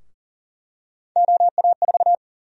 Morse code, heard as a keyed tone, about 40 wpm, 700 Hz. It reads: OA4
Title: USA Call Sign Prefixes - Repeated 2x - 40wpm